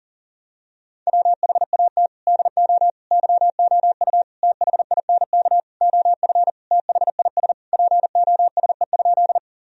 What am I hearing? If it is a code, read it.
Morse code, heard as a keyed tone, about 40 wpm, 700 Hz. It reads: WHAT DO YOU THINK OF THIS POSE?
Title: Sentences from Top 1000 Words - Review Part 2 - 40wpm